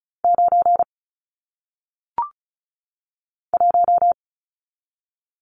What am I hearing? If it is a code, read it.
Morse code, heard as a keyed tone, about 36 wpm, 700 Hz. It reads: 9 1
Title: Numbers - 1-Digit 35wpm